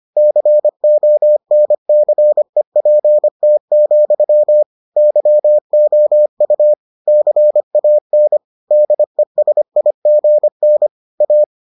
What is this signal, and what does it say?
Morse code, heard as a keyed tone, about 25 wpm, 600 Hz. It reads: CONCEPT, YOU CAN DESIGN A